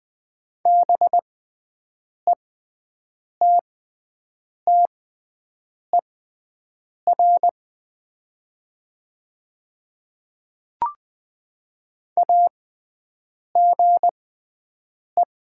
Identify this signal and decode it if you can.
Morse code, heard as a keyed tone, about 20 wpm, spaced as for 7 wpm, 700 Hz. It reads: BETTER AGE
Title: 6x Character Spacing - Top 500 Words - 20wpm